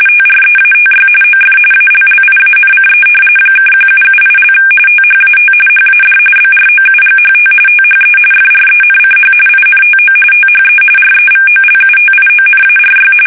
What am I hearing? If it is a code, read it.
RTTY, about 45.45 baud. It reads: TREATY OF ALLIANCE (1778) THAT HAD BROUGHT THE FRENCH INTO THE REVOLUTIONARY
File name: fsk850 / usnavy1